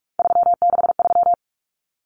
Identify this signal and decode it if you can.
Morse code, heard as a keyed tone, about 44 wpm, 700 Hz. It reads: 363
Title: Common Cut Numbers - 3-Digit 45wpm